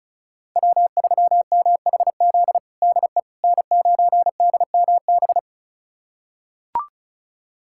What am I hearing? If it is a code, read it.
Morse code, heard as a keyed tone, about 36 wpm, 700 Hz. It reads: W3MHZ DE N9DMB
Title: USA Callsign DE Callsign - 5 Characters - 35wpm